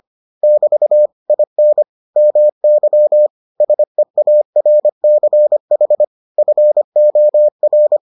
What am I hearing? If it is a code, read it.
Morse code, heard as a keyed tone, about 25 wpm, 600 Hz. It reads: <BT> IN MY SEARCH FOR